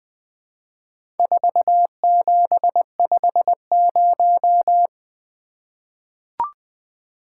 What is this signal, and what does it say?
Morse code, heard as a keyed tone, about 20 wpm, 700 Hz. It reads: 4750
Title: Four Letter-Number Combinations - 20wpm